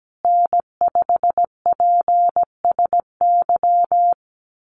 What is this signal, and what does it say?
Morse code, heard as a keyed tone, about 17 wpm, 700 Hz. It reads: N5PSY
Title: USA Call Signs Repeated 2x 17wpm